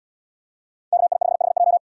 Morse code, heard as a keyed tone, about 72 wpm, 700 Hz. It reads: KI5HJ